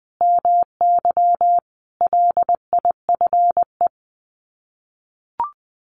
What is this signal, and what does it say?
Morse code, heard as a keyed tone, about 20 wpm, 700 Hz. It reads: MY LIFE
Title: Sets of 5 Words 20wpm